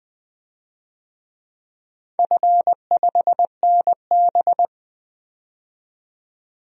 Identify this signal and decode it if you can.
Morse code, heard as a keyed tone, about 20 wpm, 700 Hz. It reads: F5NB
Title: DXCC-100 Call Signs - 4 Characters - 20wpm